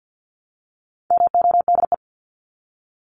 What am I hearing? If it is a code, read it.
Morse code, heard as a keyed tone, about 50 wpm, 700 Hz. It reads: MODE
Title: Top 1751-2000 Words 50wpm